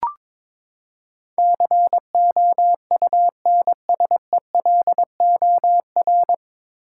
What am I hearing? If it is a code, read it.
Morse code, heard as a keyed tone, about 21 wpm, 700 Hz. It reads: COUNSELOR
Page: Morse Code Ninja practice 22wpm